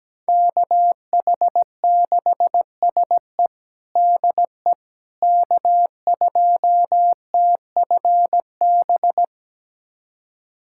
Morse code, heard as a keyed tone, about 17 wpm, 700 Hz. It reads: KH6SE DE K2TFB
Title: USA Callsign DE Callsign - 5 Characters - 17wpm